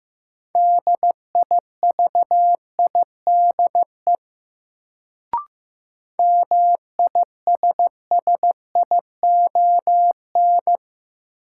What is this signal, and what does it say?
Morse code, heard as a keyed tone, about 15 wpm, 700 Hz. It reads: DIVIDE MISSION
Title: Sentences from Top 1600 Words - New Word Drill - 15wpm